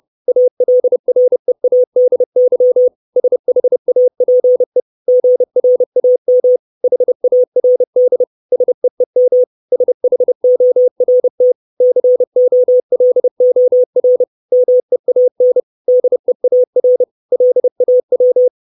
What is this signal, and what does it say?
Morse code, heard as a keyed tone, about 30 wpm, 500 Hz. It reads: ALREADY SHAPE GRAM HARD SEEM SHORT COLOR MEAN DEAR LAW
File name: 632 Words 30WPM